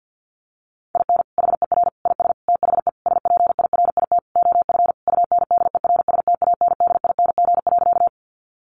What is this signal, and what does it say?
Morse code, heard as a keyed tone, about 50 wpm, 700 Hz. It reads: IN HER IS THE SPIRIT OF UNDERSTANDING.
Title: Sentences from Top 1400 Words 50wpm